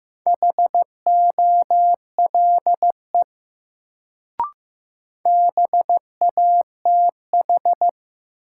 Morse code, heard as a keyed tone, about 15 wpm, 700 Hz. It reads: HOLE BATH
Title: Sentences from Top 1600 Words - New Word Drill - 15wpm